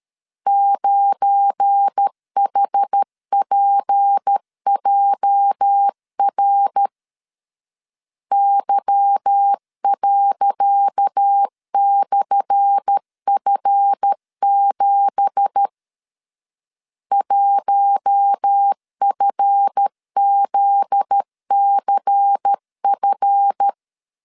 Morse code, heard as a keyed tone, about 13 wpm, 800 Hz. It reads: 9HPJR Y./F7 1FZCF
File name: LEZIONI / 10 SIGNS / 10C DOT 12wpm 2451